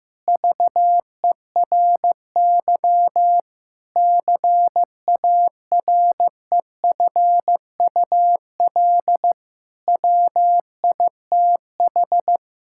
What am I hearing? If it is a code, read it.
Morse code, heard as a keyed tone, about 15 wpm, 700 Hz. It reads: VERY CAREFUL WITH
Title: Sentences from Top 1600 Words 15wpm